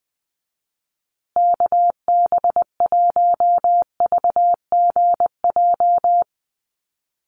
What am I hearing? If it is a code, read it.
Morse code, heard as a keyed tone, about 20 wpm, 700 Hz. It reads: KB1VGJ
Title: USA Call Signs Repeated 2x 20wpm